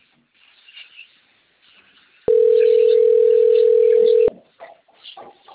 Ringback tone (North America), 440 + 480 Hz.